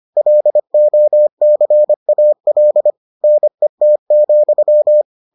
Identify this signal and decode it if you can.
Morse code, heard as a keyed tone, about 25 wpm, 600 Hz. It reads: LOCAL NET,